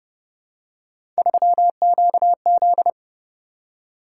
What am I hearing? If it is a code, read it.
Morse code, heard as a keyed tone, about 27 wpm, 700 Hz. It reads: 3QZ